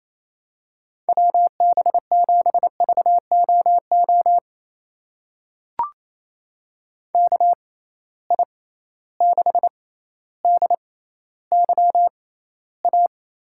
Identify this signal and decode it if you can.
Morse code, heard as a keyed tone, about 28 wpm, spaced as for 14 wpm, 700 Hz. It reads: WB7VOO KI6DYA
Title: USA Call Signs - 6 Characters - 6x Character Spacing - 28wpm